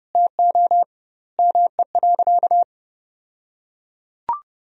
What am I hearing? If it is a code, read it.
Morse code, heard as a keyed tone, about 30 wpm, 700 Hz. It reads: TO ME.
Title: 2x Word-Spacing - Sentences from Top 300 Words - 30wpm